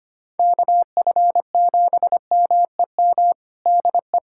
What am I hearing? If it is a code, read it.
Morse code, heard as a keyed tone, about 25 wpm, 700 Hz. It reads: KF7MEM DE
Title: USA Callsign DE Callsign - 6 Characters - 25wpm